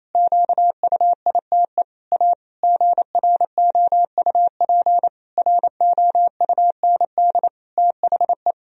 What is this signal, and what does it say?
Morse code, heard as a keyed tone, about 28 wpm, 700 Hz. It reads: QUITE A GROUP ROUND THE
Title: Sentences from Top 1000 Words - Review Part 1 - 28wpm